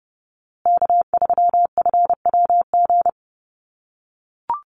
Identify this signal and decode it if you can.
Morse code, heard as a keyed tone, about 28 wpm, 700 Hz. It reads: K3FWG
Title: USA Call Signs Repeated 3x 30wpm